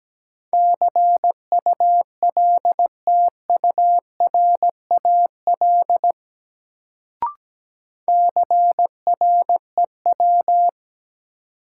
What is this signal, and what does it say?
Morse code, heard as a keyed tone, about 17 wpm, 700 Hz. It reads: CULTURAL CREW
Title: Sentences from Top 1500 Words - New Word Drill - 17wpm